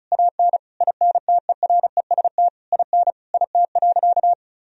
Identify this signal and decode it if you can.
Morse code, heard as a keyed tone, about 36 wpm, 700 Hz. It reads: AN INTEREST IN IT.
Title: Sentences from Top 1000 Words - Review Part 2 - 35wpm